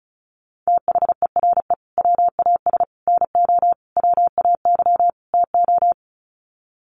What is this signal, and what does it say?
Morse code, heard as a keyed tone, about 34 wpm, 700 Hz. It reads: THERE WAS NO WAY TO
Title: Sets of 5 Words Encore 35wpm